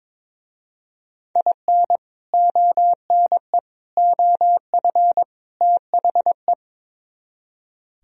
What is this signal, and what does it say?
Morse code, heard as a keyed tone, about 22 wpm, 700 Hz. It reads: IN ONE OF THE
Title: Speed-Racing - Sets of 4 Words 22wpm